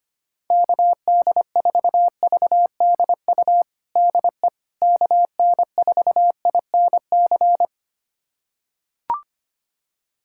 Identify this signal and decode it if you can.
Morse code, heard as a keyed tone, about 25 wpm, 700 Hz. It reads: KD4VDU DE KN4INC
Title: USA Callsign DE Callsign - 6 Characters - 25wpm